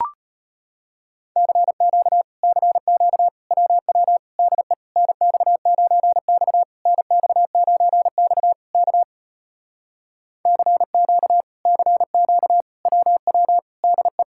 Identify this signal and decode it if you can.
Morse code, heard as a keyed tone, about 33 wpm, 700 Hz. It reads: CQ CQ WW DE NX9X NX9X K CQ CQ WW DE